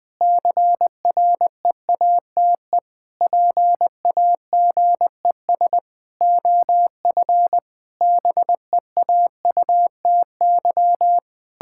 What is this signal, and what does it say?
Morse code, heard as a keyed tone, about 20 wpm, 700 Hz. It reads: CREATE PAGES OF BEAUTY
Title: Pangram Challenge v2 - 20wpm